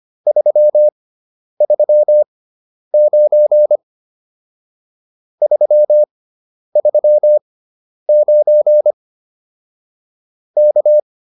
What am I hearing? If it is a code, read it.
Morse code, heard as a keyed tone, about 25 wpm, spaced as for 10 wpm, 600 Hz. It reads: 339 339 K